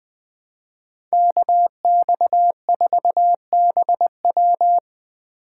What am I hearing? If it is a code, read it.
Morse code, heard as a keyed tone, about 24 wpm, 700 Hz. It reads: KX4BW
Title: Speed Racing - USA Call Signs 20wpm